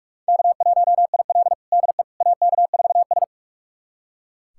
Morse code, heard as a keyed tone, about 44 wpm, 700 Hz. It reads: K1IL DE AK4S
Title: USA Callsign DE Callsign - 4 Characters - 45wpm